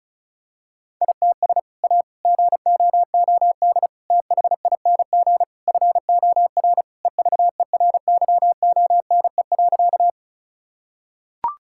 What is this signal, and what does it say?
Morse code, heard as a keyed tone, about 34 wpm, 700 Hz. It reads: ITS A GOOD THING FOR EVERYONE.
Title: Sentences from Top 1000 Words - Review Part 5 - 35wpm